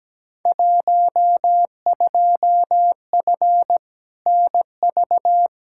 Morse code, heard as a keyed tone, about 17 wpm, 700 Hz. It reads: 12F NV
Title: Field Day Exchanges Encore - 17wpm